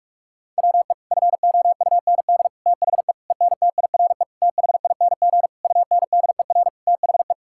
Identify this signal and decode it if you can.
Morse code, heard as a keyed tone, about 41 wpm, 700 Hz. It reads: WE FOUND THE ENTIRE THING UNDER THE
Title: Sentences from Top 1000 Words - Review Part 5 - 45wpm